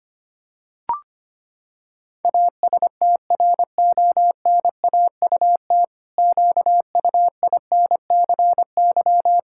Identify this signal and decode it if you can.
Morse code, heard as a keyed tone, about 25 wpm, 700 Hz. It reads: ASTRONAUT QUINCY
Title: Pangram Challenge v2 - 25wpm